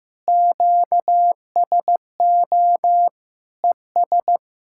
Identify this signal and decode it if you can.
Morse code, heard as a keyed tone, about 15 wpm, 700 Hz. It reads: QSO ES
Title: Practice QSO - Exchange 4 - Ending - 15wpm